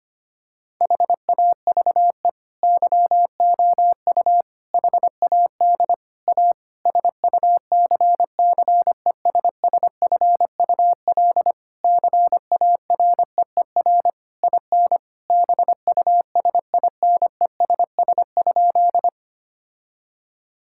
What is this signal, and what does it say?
Morse code, heard as a keyed tone, about 25 wpm, 700 Hz. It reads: HAVE YOU HAD A SUCCESSFUL CAREER IN BUSINESS?
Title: Sentences from Top 1000 Words - Review Part 6 - 25wpm